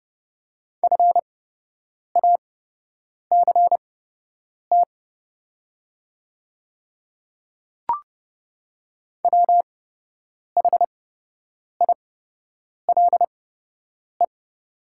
Morse code, heard as a keyed tone, about 30 wpm, spaced as for 8 wpm, 700 Hz. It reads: FACT WHILE
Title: 8x Character Spacing - Top 500 Words - 30wpm